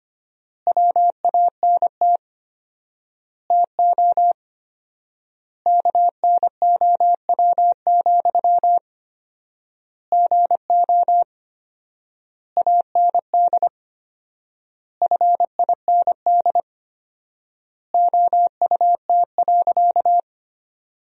Morse code, heard as a keyed tone, about 25 wpm, 700 Hz. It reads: WANT TO KNOW, GO AND FIND OUT.
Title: 4x Word-Spacing - Sentences from Top 200 Words - 25wpm